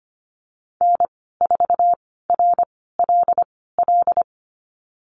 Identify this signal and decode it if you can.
Morse code, heard as a keyed tone, about 25 wpm, spaced as for 16 wpm, 700 Hz. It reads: N4RLL
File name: Lesson 15g - Number 5 - Call Signs 25wpm